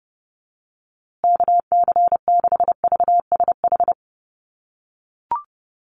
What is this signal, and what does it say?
Morse code, heard as a keyed tone, about 30 wpm, 700 Hz. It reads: KC6VSH
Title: USA Call Signs Repeated 2x 30wpm